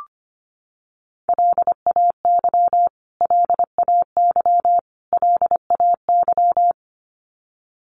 Morse code, heard as a keyed tone, about 25 wpm, 700 Hz. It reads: LAY LAY LAY
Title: Top 500 Words Repeated 25wpm